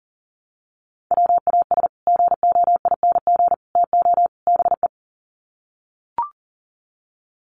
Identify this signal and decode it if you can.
Morse code, heard as a keyed tone, about 40 wpm, 700 Hz. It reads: WAS GOING TO BE
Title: Sets of 4 Words 40wpm